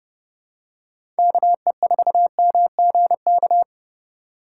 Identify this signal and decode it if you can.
Morse code, heard as a keyed tone, about 30 wpm, 700 Hz. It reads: KE4MGK
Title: USA Call Signs - 6 Characters - 8x Character Spacing - 30wpm